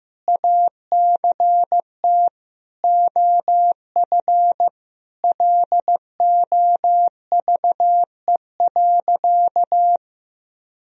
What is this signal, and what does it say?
Morse code, heard as a keyed tone, about 15 wpm, 700 Hz. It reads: ACT OF LOVE.
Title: Sentences from Top 1000 Words - Review Part 3 - 15wpm